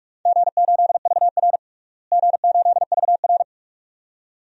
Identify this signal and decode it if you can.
Morse code, heard as a keyed tone, about 46 wpm, 700 Hz. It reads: G8VR G8VR
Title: DXCC-100 Call Signs - 4 Characters - Repeated 2x - 45wpm